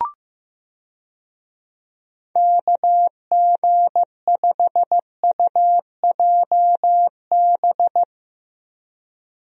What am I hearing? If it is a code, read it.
Morse code, heard as a keyed tone, about 15 wpm, 700 Hz. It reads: KG5UJB
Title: USA Call Signs - 6 Characters - Repeated 3x - 15wpm